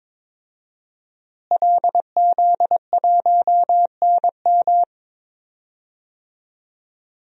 Morse code, heard as a keyed tone, about 21 wpm, 700 Hz. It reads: LZ1NM
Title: DXCC-100 Call Signs - 5 Characters - Repeated 3x - 22wpm